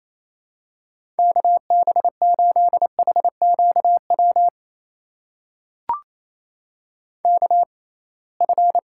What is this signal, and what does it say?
Morse code, heard as a keyed tone, about 28 wpm, spaced as for 21 wpm, 700 Hz. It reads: KB8HQW KF
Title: USA Call Signs - 6 Characters - 6x Character Spacing - 28wpm